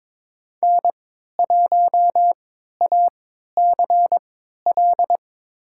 Morse code, heard as a keyed tone, about 22 wpm, spaced as for 12 wpm, 700 Hz. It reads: N1ACL
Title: USA Call Signs - 5 Characters - 3x Character Spacing - 22wpm